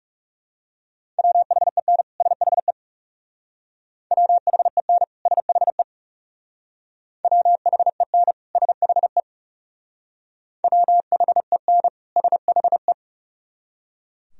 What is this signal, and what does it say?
Morse code, heard as a keyed tone, about 39 wpm, 700 Hz. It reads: WHEN SHE WHEN SHE WHEN SHE WHEN SHE